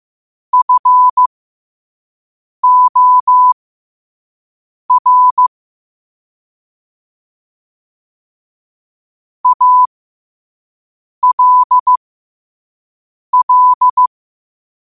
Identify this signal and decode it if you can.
Morse code, heard as a keyed tone, about 15 wpm, spaced as for 5 wpm, 1000 Hz. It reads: FOR ALL
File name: qotdfiles / 5wpm